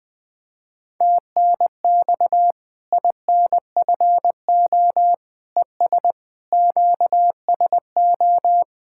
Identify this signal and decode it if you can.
Morse code, heard as a keyed tone, about 21 wpm, 700 Hz. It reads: TNX INFO ES QSO
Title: K1USN SST QSO Practice 20wpm